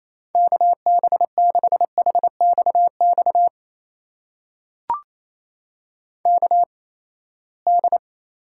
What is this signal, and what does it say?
Morse code, heard as a keyed tone, about 28 wpm, spaced as for 19 wpm, 700 Hz. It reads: KB6HXX KD
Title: USA Call Signs - 6 Characters - 8x Character Spacing - 28wpm